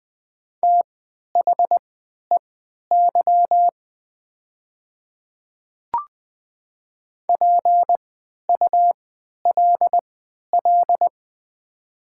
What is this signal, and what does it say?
Morse code, heard as a keyed tone, about 20 wpm, spaced as for 11 wpm, 700 Hz. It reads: THEY PULL
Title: 3x Character Spacing - Top 500 Words - 20wpm